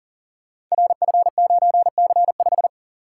Morse code, heard as a keyed tone, about 38 wpm, 690 Hz. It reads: RF9C5
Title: Code Groups - 5 Letter:Numbers - 40wpm